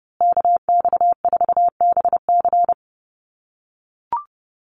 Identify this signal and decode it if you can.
Morse code, heard as a keyed tone, about 30 wpm, 700 Hz. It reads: KX4BC
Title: USA Call Signs Repeated 2x 30wpm